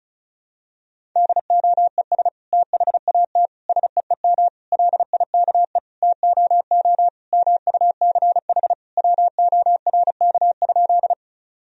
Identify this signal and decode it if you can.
Morse code, heard as a keyed tone, about 34 wpm, 700 Hz. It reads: DOES THAT SEEM LIKE TOO MUCH WORK?
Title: Sentences from Top 1000 Words - Review Part 1 - 35wpm